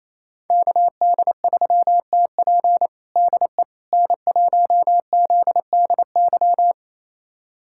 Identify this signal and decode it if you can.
Morse code, heard as a keyed tone, about 27 wpm, 700 Hz. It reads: KD3TP DE N1ZDY
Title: USA Callsign DE Callsign - 5 Characters - 28wpm